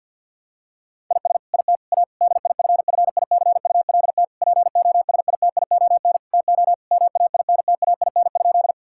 Morse code, heard as a keyed tone, about 50 wpm, 700 Hz. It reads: IS IT A DIFFICULT POSITION TO MAINTAIN?
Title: Sentences from Top 1000 Words - Review Part 5 - 50wpm